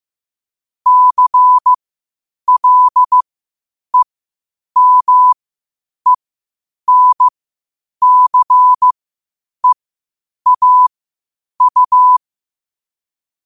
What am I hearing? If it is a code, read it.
Morse code, heard as a keyed tone, about 15 wpm, spaced as for 8 wpm, 1000 Hz. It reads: CLEMENCEAU
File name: qotdfiles / 8wpm